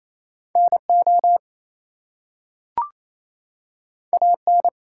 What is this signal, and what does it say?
Morse code, heard as a keyed tone, about 27 wpm, 700 Hz. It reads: NO AN
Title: Two Letter Words - 28wpm